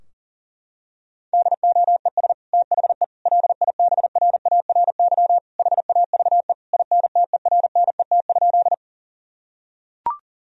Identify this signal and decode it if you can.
Morse code, heard as a keyed tone, about 40 wpm, 700 Hz. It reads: DOES THE LIBRARY HAVE INTERNET?
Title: Sentences from Top 1000 Words - Review Part 4 - 40wpm